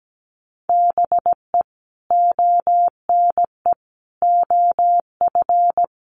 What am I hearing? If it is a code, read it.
Morse code, heard as a keyed tone, about 17 wpm, 700 Hz. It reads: BE ONE OF
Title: Sets of 4 Words 17wpm